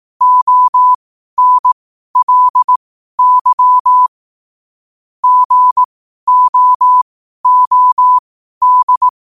Morse code, heard as a keyed tone, about 18 wpm, spaced as for 12 wpm, 1000 Hz. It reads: ONLY GOOD
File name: qotdfiles / 12wpm